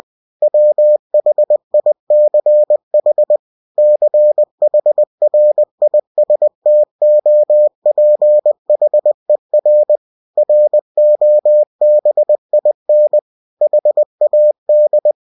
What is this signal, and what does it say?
Morse code, heard as a keyed tone, about 20 wpm, 600 Hz. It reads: WHICH CHRISTOPHER ROBIN HAD
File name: cw-books / winnie-the-pooh / winnie-20wpm-page-0092